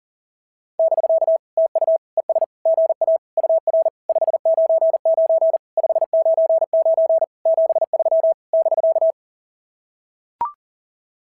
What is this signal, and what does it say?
Morse code, heard as a keyed tone, about 40 wpm, 650 Hz. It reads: <BK> TU ES GA UR 599 599 73 <BK>
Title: SOTA QSO Practice 40wpm